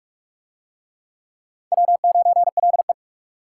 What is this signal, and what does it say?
Morse code, heard as a keyed tone, about 44 wpm, 700 Hz. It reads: W9LE